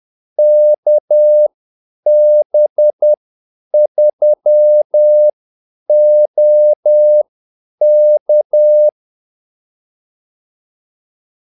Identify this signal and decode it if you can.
Morse code, heard as a keyed tone, about 10 wpm, spaced as for 8 wpm, 600 Hz. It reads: KB3OK
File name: Callsigns-10-8wpm